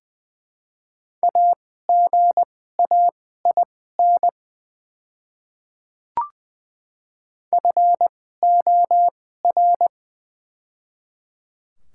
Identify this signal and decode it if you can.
Morse code, heard as a keyed tone, about 20 wpm, spaced as for 14 wpm, 700 Hz. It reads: AGAIN FOR
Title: 2x Character Spacing - Top 300 Words - 20wpm